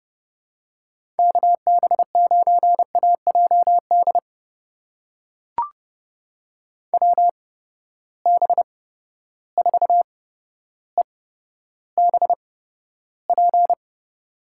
Morse code, heard as a keyed tone, about 30 wpm, spaced as for 13 wpm, 700 Hz. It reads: KB9AJD WB4EBP
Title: USA Call Signs - 8x Character Spacing - 30wpm